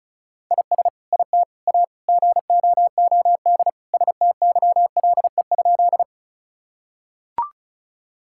Morse code, heard as a keyed tone, about 34 wpm, 700 Hz. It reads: IS IT A GOOD STYLE?
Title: Sentences from Top 1000 Words - Review Part 5 - 35wpm